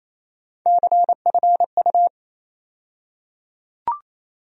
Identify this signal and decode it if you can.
Morse code, heard as a keyed tone, about 28 wpm, 700 Hz. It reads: CFU
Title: USA Call Sign Postfixes - 3 Letters 28wpm